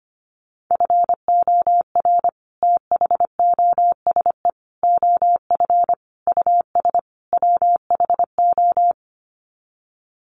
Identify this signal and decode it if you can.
Morse code, heard as a keyed tone, about 25 wpm, 700 Hz. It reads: FOR THOSE OF US WHO
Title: Sets of 5 Words Encore 25wpm